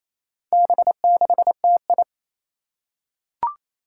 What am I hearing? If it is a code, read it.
Morse code, heard as a keyed tone, about 28 wpm, 700 Hz. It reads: B6TI
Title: Four Challenging Letter-Number Combinations 28wpm